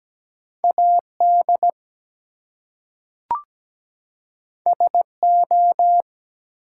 Morse code, heard as a keyed tone, about 17 wpm, 700 Hz. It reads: AD SO
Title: Two Letter Words - 17wpm